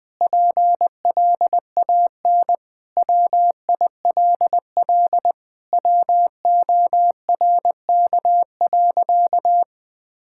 Morse code, heard as a keyed tone, about 20 wpm, 700 Hz. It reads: PLAN WILL WORK.
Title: Sentences with 7-Word Phrases - 20wpm